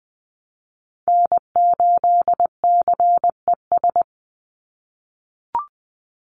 Morse code, heard as a keyed tone, about 19 wpm, 700 Hz. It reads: N8CES
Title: USA Call Signs 20wpm